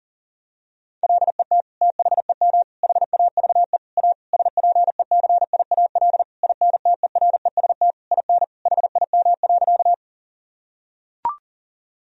Morse code, heard as a keyed tone, about 40 wpm, 700 Hz. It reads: LET THEM HAVE A SPECIAL INTEREST IN HIM.
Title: Sentences from Top 1000 Words - Review Part 3 - 40wpm